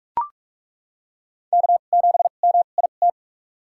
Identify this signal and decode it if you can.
Morse code, heard as a keyed tone, about 42 wpm, spaced as for 31 wpm, 700 Hz. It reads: K7MIT
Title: USA Call Signs - 2x Character Spacing - 45wpm